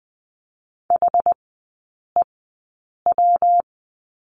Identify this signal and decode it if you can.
Morse code, heard as a keyed tone, about 20 wpm, spaced as for 8 wpm, 690 Hz. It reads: HEW
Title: 145 Lesson 24e - Letter y - Three Characters 20wpm